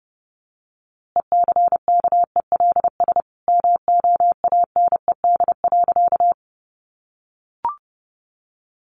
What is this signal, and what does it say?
Morse code, heard as a keyed tone, about 30 wpm, 700 Hz. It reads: ECKELS MOANED.
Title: A Sound Of Thunder 30wpm